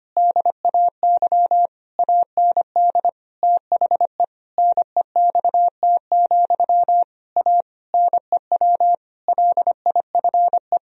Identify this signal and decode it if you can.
Morse code, heard as a keyed tone, about 25 wpm, 700 Hz. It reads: DAY AND THE NEXT, A NEW LIFE
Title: Sentences from Top 1000 Words - Review Part 1 - 25wpm